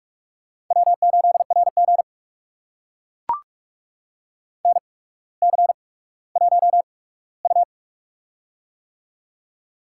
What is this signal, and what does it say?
Morse code, heard as a keyed tone, about 42 wpm, spaced as for 19 wpm, 700 Hz. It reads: W8RG NC1U
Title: USA Call Signs - 4 Characters - 8x Character Spacing - 45wpm